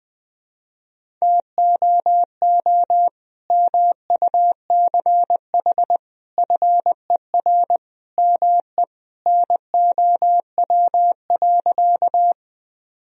Morse code, heard as a keyed tone, about 21 wpm, 700 Hz. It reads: TOO MUCH FER ME NOW.
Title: Sentences from Top 200 Words and Abbreviations 20wpm